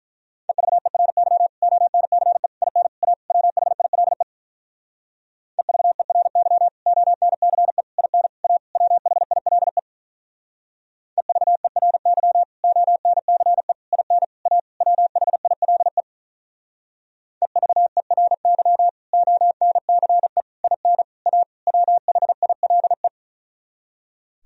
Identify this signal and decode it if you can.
Morse code, heard as a keyed tone, about 44 wpm, 700 Hz. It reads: EVERY ONCE IN A WHILE EVERY ONCE IN A WHILE EVERY ONCE IN A WHILE EVERY ONCE IN A WHILE
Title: Speed-Racing - Sets of 5 Words 35wpm